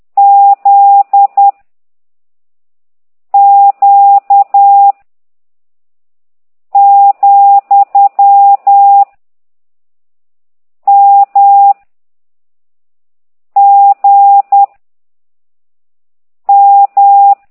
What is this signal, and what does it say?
Morse code, heard as a keyed tone, about 10 wpm, 800 Hz. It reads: Z Q , M G M